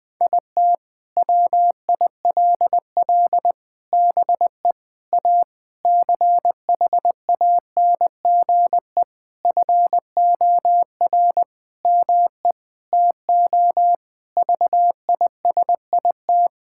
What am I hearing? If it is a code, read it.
Morse code, heard as a keyed tone, about 20 wpm, 700 Hz. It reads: IT WILL BE A CHANGE FOR ME TO VISIT
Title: Sentences from Top 1000 Words - Review Part 6 - 20wpm